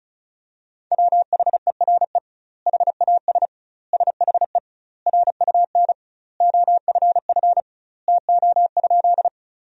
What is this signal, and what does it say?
Morse code, heard as a keyed tone, about 36 wpm, 700 Hz. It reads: WHERE HAS SHE RUN OFF TO?
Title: 2x Word-Spacing - Sentences from Top 300 Words - 35wpm